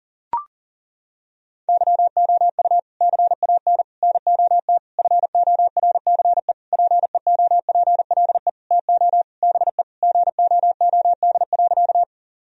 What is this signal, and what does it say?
Morse code, heard as a keyed tone, about 40 wpm, 700 Hz. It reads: YOU CAN NOT FORCE PEOPLE TO BE GOOD.
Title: Sentences from Top 1000 Words - Review Part 3 - 40wpm